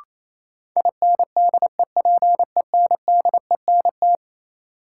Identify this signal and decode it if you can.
Morse code, heard as a keyed tone, about 28 wpm, 700 Hz. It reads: INDEPENDENT
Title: Sentences from Top 900 Words - New Word Drill - 28wpm